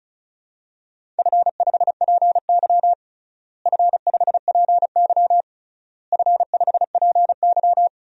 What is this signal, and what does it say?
Morse code, heard as a keyed tone, about 34 wpm, 700 Hz. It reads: F5PY F5PY F5PY
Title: DXCC-100 Call Signs - 4 Characters - Repeated 3x - 35wpm